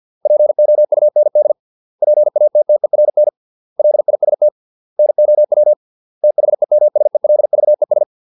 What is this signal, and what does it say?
Morse code, heard as a keyed tone, about 50 wpm, 600 Hz. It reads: POUND PATTERN LIST NOW THEMSELVES